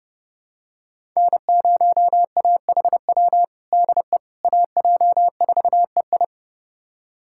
Morse code, heard as a keyed tone, about 30 wpm, 700 Hz. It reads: N0AHW DE AJ4EI
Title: USA Callsign DE Callsign - 5 Characters - 30wpm